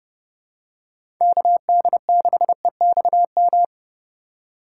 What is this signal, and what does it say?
Morse code, heard as a keyed tone, about 30 wpm, 700 Hz. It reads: KD6EXM
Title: USA Call Signs - 4x Character Spacing - 30wpm